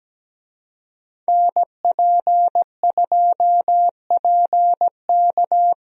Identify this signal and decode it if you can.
Morse code, heard as a keyed tone, about 17 wpm, 700 Hz. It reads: NP2PK
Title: USA Call Signs - 5 Characters - Repeated 2x - 17wpm